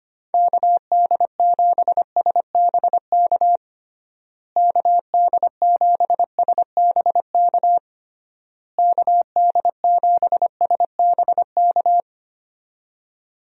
Morse code, heard as a keyed tone, about 25 wpm, 700 Hz. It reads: KD7SBK KD7SBK KD7SBK
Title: USA Call Signs - 6 Characters - Repeated 3x - 25wpm